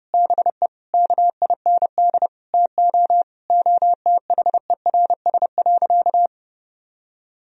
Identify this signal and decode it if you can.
Morse code, heard as a keyed tone, about 32 wpm, 700 Hz. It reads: BE KIND TO OTHERS.
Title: Sentences from Top 1000 Words - Review Part 4 - 30wpm